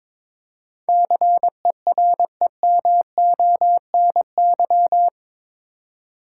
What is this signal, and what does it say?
Morse code, heard as a keyed tone, about 22 wpm, 700 Hz. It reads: CEREMONY
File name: practice / 22wpm / Eight Letter Words - 22wpm